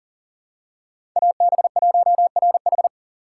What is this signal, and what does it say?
Morse code, heard as a keyed tone, about 37 wpm, 700 Hz. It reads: AB1RH
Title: USA Call Signs - 5 Characters - 2x Character Spacing - 40wpm